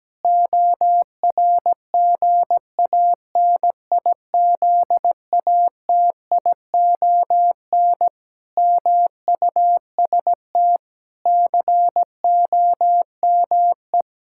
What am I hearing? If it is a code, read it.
Morse code, heard as a keyed tone, about 17 wpm, 700 Hz. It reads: ORGANIZATION MUST COME
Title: Sentences from Top 1000 Words - Review Part 5 - 17wpm